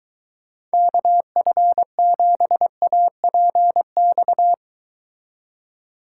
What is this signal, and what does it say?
Morse code, heard as a keyed tone, about 23 wpm, 700 Hz. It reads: KF7APX